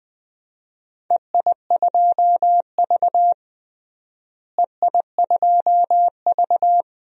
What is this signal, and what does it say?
Morse code, heard as a keyed tone, about 20 wpm, 700 Hz. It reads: EI2V EI2V
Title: DXCC-100 Call Signs - 4 Characters - Repeated 3x - 20wpm